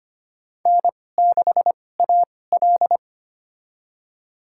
Morse code, heard as a keyed tone, about 25 wpm, spaced as for 18 wpm, 700 Hz. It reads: N6AL
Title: USA Call Signs - 2x Character Spacing - 25wpm